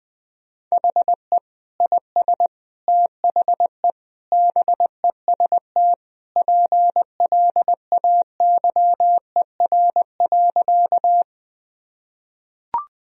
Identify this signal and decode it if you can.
Morse code, heard as a keyed tone, about 20 wpm, 700 Hz. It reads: HE IS THE BEST PLAYER.
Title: Sentences from Top 1000 Words - Review Part 4 - 20wpm